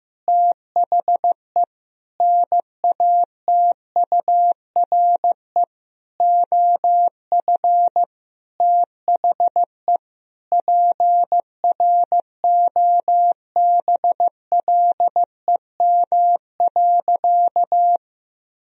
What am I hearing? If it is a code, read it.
Morse code, heard as a keyed tone, about 15 wpm, 700 Hz. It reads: THE NATURE OF THE PROBLEM.
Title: Sentences from Top 1000 Words - Review Part 2 - 15wpm